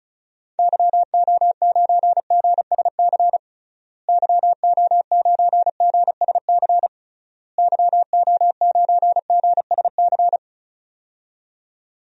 Morse code, heard as a keyed tone, about 34 wpm, 700 Hz. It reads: YO9GSC YO9GSC YO9GSC
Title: DXCC-100 Call Signs - 6 Characters - Repeated 3x - 35wpm